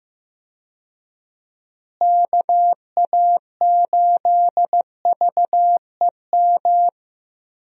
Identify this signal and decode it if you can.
Morse code, heard as a keyed tone, about 15 wpm, 700 Hz. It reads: KA8VEM
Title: USA Call Signs - 6 Characters - Repeated 2x - 15wpm